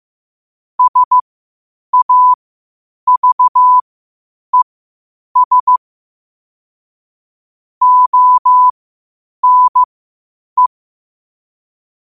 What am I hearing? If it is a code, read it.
Morse code, heard as a keyed tone, about 15 wpm, spaced as for 8 wpm, 1000 Hz. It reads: SAVES ONE
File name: qotdfiles / 8wpm